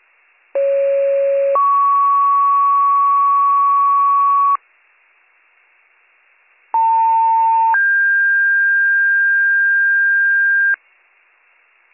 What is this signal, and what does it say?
Two-tone page, heard as a one-second tone, then a three-second tone, about 550 then 1120 Hz, about 900 then 1600 Hz. It is heard from 0.6 to 10.8 s.